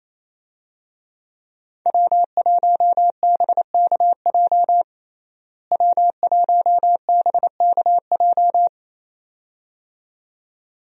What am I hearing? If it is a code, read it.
Morse code, heard as a keyed tone, about 28 wpm, 700 Hz. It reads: W1BKJ W1BKJ
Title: USA Call Signs - 5 Characters - Repeated 2x - 28wpm